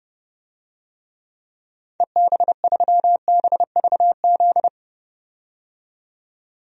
Morse code, heard as a keyed tone, about 30 wpm, 700 Hz. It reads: EB3BVZ
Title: DXCC-100 Call Signs - 6 Characters - 30wpm